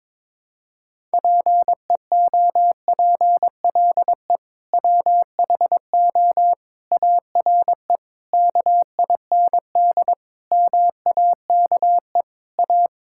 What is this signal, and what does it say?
Morse code, heard as a keyed tone, about 22 wpm, 700 Hz. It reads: PEOPLE WHO ARE KIND MAKE A
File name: practice / 22wpm / Sentences with 3-Word Phrases - 22wpm